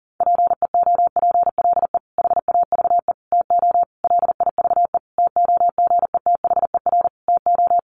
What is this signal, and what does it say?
Morse code, heard as a keyed tone, about 40 wpm, 700 Hz. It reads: PEOPLE HAVE TO LIVE TOGETHER TO
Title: Sentences from Top 1100 Words 40wpm